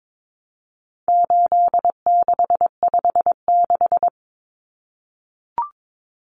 Morse code, heard as a keyed tone, about 22 wpm, 700 Hz. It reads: 8656
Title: Common Cut Numbers - 4-Digit 22wpm